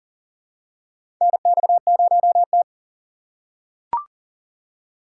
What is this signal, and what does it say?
Morse code, heard as a keyed tone, about 40 wpm, 700 Hz. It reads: NX0T
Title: USA Call Signs - 4 Characters - 2x Character Spacing - 40wpm